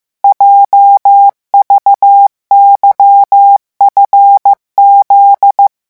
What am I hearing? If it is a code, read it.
Morse code, heard as a keyed tone, about 15 wpm, 800 Hz. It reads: JVYFZ
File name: morse mp3 / revision11 lettres vitesse 14